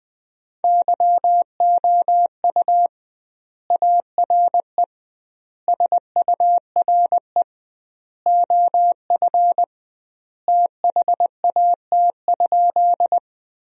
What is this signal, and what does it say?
Morse code, heard as a keyed tone, about 20 wpm, 700 Hz. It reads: YOU ARE SURE OF THAT?
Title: 2x Word-Spacing - Sentences from Top 300 Words - 20wpm